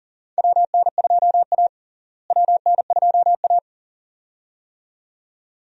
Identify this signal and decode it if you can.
Morse code, heard as a keyed tone, about 40 wpm, 700 Hz. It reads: WN2A WN2A
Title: USA Call Signs - 4 Characters - Repeated 2x - 40wpm